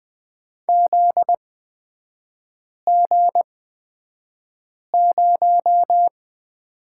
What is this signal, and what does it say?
Morse code, heard as a keyed tone, about 20 wpm, spaced as for 5 wpm, 700 Hz. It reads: ZG0
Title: 243 Lesson 39e - Letter z - Three Characters 20wpm